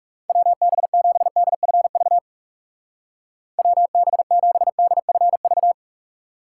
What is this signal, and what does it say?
Morse code, heard as a keyed tone, about 41 wpm, 700 Hz. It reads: WB7DFV WB7DFV